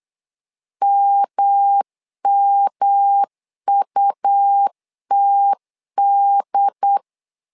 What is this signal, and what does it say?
Morse code, heard as a keyed tone, about 8 wpm, 800 Hz. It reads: MMUTD
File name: LEZIONI / 3 MWRO / 3 M 8wpm 2300